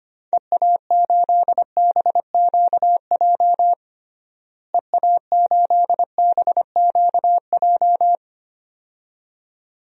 Morse code, heard as a keyed tone, about 25 wpm, 700 Hz. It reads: EA8BQJ EA8BQJ
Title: DXCC-100 Call Signs - 6 Characters - Repeated 2x - 25wpm